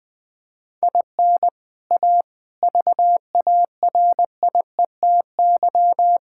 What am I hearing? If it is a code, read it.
Morse code, heard as a keyed tone, about 21 wpm, 700 Hz. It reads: IN A VARIETY
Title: Build A Phrase - 4-Word Phrases - 20wpm